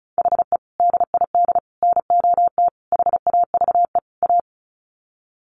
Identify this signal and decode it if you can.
Morse code, heard as a keyed tone, about 34 wpm, 700 Hz. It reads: HE DID NOT HAVE A
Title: Sets of 5 Words Encore 35wpm